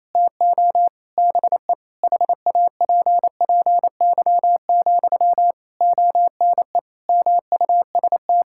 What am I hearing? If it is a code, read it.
Morse code, heard as a keyed tone, about 28 wpm, 700 Hz. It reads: TO BE HAPPY, ONE MUST
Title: Sentences from Top 1000 Words - Review Part 6 - 28wpm